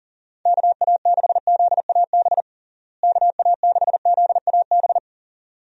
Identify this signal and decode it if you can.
Morse code, heard as a keyed tone, about 40 wpm, 700 Hz. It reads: KA6ZAB KA6ZAB
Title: USA Call Signs - 6 Characters - Repeated 2x - 40wpm